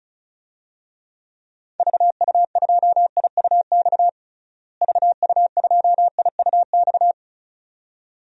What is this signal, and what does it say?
Morse code, heard as a keyed tone, about 34 wpm, 700 Hz. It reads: VU2IUX VU2IUX
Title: DXCC-100 Call Signs - 6 Characters - Repeated 2x - 35wpm